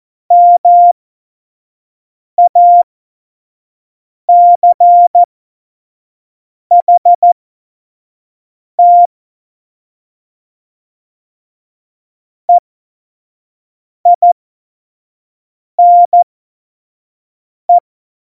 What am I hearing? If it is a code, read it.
Morse code, heard as a keyed tone, about 14 wpm, spaced as for 5 wpm, 700 Hz. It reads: MACHT EINE
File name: Morsekurs / Witze / Witz 04 Juli 17, Tempo 25 F70